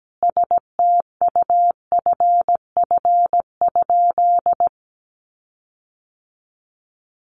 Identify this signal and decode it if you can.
Morse code, heard as a keyed tone, about 17 wpm, 700 Hz. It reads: STUFF?
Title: Sentences from Top 1400 Words 17wpm